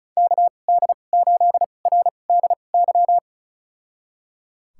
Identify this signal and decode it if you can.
Morse code, heard as a keyed tone, about 34 wpm, spaced as for 25 wpm, 700 Hz. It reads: KD8RDY
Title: USA Call Signs - 6 Characters - 2x Character Spacing - 35wpm